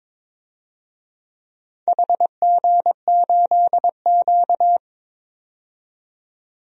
Morse code, heard as a keyed tone, about 22 wpm, 700 Hz. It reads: HG8Q
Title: DXCC-100 Call Signs - 4 Characters - 22wpm